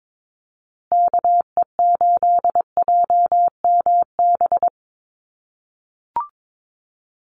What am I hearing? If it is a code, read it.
Morse code, heard as a keyed tone, about 22 wpm, 700 Hz. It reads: KE8JMB
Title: USA Call Signs Repeated 3x 22wpm